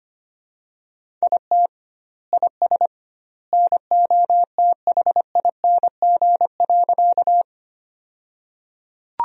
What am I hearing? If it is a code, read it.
Morse code, heard as a keyed tone, about 25 wpm, 700 Hz. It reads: IT IS NOTHING.
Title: 2x Word-Spacing - Sentences from Top 500 Words - 25wpm